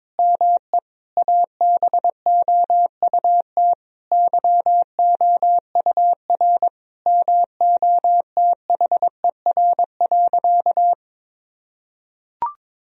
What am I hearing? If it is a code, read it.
Morse code, heard as a keyed tone, about 22 wpm, 700 Hz. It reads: ME ABOUT YOUR MOTHER.
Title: Sentences from Top 1000 Words - Review Part 3 - 22wpm